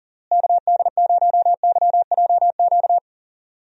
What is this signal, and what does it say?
Morse code, heard as a keyed tone, about 40 wpm, 700 Hz. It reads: KD0YJQ
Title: USA Call Signs - 6 Characters - Repeated 3x - 40wpm